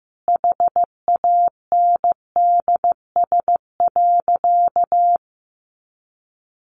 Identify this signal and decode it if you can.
Morse code, heard as a keyed tone, about 15 wpm, 700 Hz. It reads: HANDS.
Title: A Sound Of Thunder 15wpm